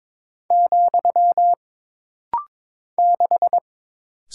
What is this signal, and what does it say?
Morse code, heard as a keyed tone, about 21 wpm, 700 Hz. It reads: , 6
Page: This rapid-fire timing is geared to push your ICR (Instant Character Recognition) skills.